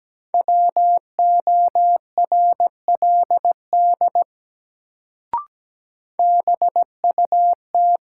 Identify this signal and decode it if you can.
Morse code, heard as a keyed tone, about 17 wpm, 700 Hz. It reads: WORLD BUT
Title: Sets of 2 Words - New Word Drill - 17wpm